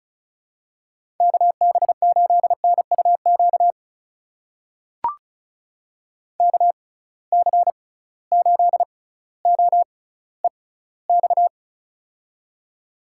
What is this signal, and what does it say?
Morse code, heard as a keyed tone, about 36 wpm, spaced as for 18 wpm, 700 Hz. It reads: KB8NUQ KC8OEX
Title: USA Call Signs - 6 Characters - 6x Character Spacing - 35wpm